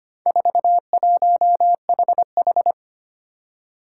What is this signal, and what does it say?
Morse code, heard as a keyed tone, about 25 wpm, 700 Hz. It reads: 41HH
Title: Four Challenging Letter-Number Combinations 25wpm